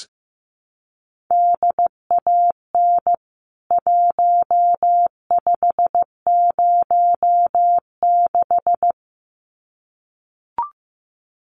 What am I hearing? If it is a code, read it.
Morse code, heard as a keyed tone, about 15 wpm, 700 Hz. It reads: DAN 1506
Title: CWT Exchange 15wpm